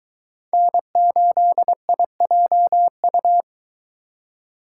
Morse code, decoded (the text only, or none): N8IJU